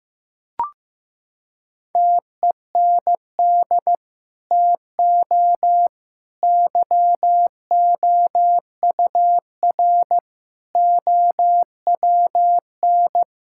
Morse code, heard as a keyed tone, about 16 wpm, 700 Hz. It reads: TEND TO YOUR OWN
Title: Sentences from Top 1000 Words - Review Part 5 - 15wpm